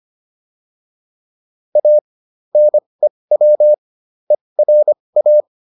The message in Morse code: A NEW ERA

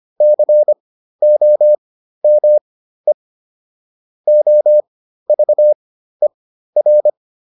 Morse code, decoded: COME OVER

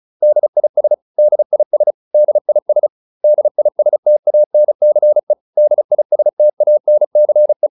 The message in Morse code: DIS DIS DIS DISTANCE DISTANCE

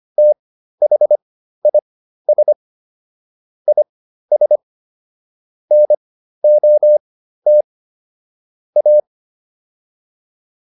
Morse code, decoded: THIS IS NOT A

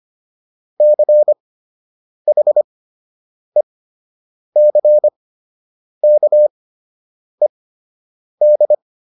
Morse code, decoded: CHECKED

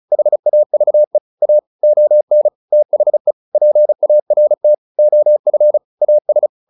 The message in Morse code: HAVE A ON THE PART OF AS

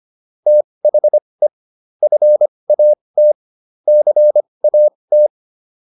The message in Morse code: THE FAT CAT

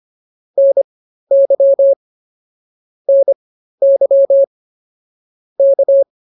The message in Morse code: NY NY K

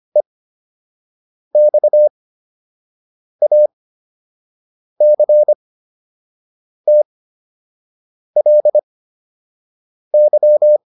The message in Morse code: EXACTLY

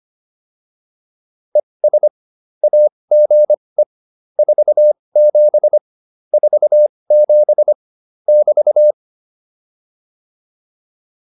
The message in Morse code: ES AGE 47 47 <BT>

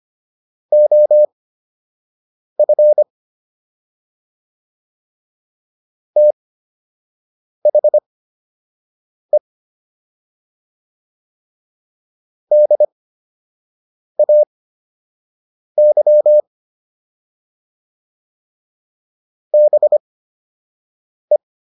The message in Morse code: OF THE DAY BE

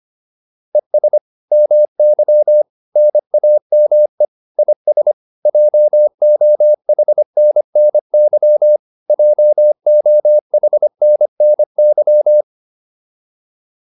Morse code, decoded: ES MY NAME IS JOHNNY JOHNNY

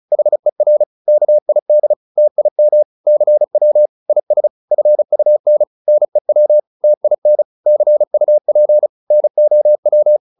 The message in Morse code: HER KID TIM CW IS FUN NEW TIN CUP NOW

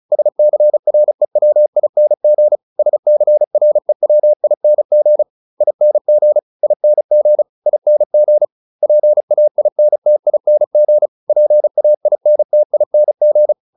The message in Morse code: SCREWING SCREWING ING ING ING PAINTING PAINTING

